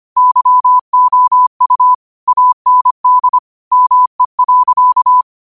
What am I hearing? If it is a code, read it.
Morse code, heard as a keyed tone, about 25 wpm, 1000 Hz. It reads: YOU AND ME.
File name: qotdfiles / 25wpm